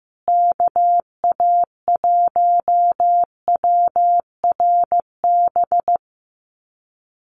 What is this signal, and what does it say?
Morse code, heard as a keyed tone, about 15 wpm, 700 Hz. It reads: KA1WRB
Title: USA Call Signs Repeated 3x 15wpm